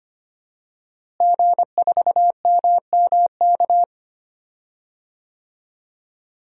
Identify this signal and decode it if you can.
Morse code, heard as a keyed tone, about 25 wpm, 700 Hz. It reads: G4MMK
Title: DXCC-100 Call Signs - 5 Characters - Repeated 3x - 25wpm